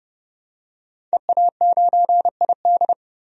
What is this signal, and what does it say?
Morse code, heard as a keyed tone, about 30 wpm, 700 Hz. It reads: EA9ID
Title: DXCC-100 Call Signs - 5 Characters - 30wpm